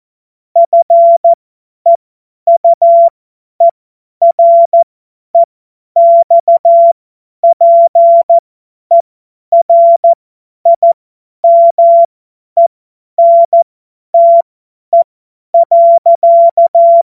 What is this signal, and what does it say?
Morse code, heard as a keyed tone, about 14 wpm, spaced as for 10 wpm, 680 Hz. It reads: FEUEREXPERIMENTE.
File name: Morsekurs / Witze / Witz 21. September 2021, 50 F70